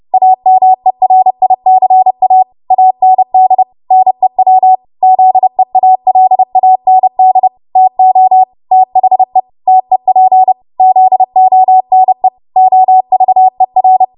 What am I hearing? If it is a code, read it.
Morse code, heard as a keyed tone, about 30 wpm, 750 Hz. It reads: AMERICA AND NEW ZEALAND TO THE TEP ZONE OVER
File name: Morse / 30WPM